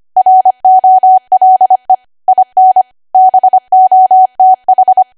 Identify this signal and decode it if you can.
Morse code, heard as a keyed tone, about 25 wpm, 750 Hz. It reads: ROLE IN BOTH
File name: Archive / 25 WPM / 25WPM